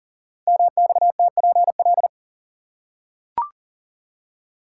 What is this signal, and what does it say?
Morse code, heard as a keyed tone, about 38 wpm, 690 Hz. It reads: MXTPL
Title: Code Groups - 5 Letters - 40wpm